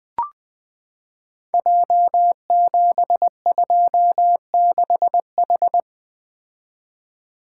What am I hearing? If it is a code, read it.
Morse code, heard as a keyed tone, about 20 wpm, 700 Hz. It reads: J726H
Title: Code Groups - 5 Letter:Numbers - 20wpm